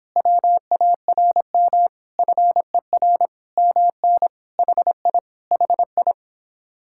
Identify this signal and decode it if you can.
Morse code, heard as a keyed tone, about 26 wpm, 700 Hz. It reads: WARM FER MN HI HI
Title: Speed-Racing Common QSO Phrases - 22wpm